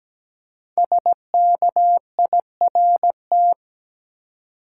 Morse code, heard as a keyed tone, about 17 wpm, 700 Hz. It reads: SKIRT
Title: Five Letter Words - 17wpm